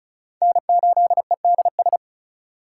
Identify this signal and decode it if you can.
Morse code, heard as a keyed tone, about 36 wpm, 700 Hz. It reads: N8EDS
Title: USA Call Signs - 5 Characters - 3x Character Spacing - 35wpm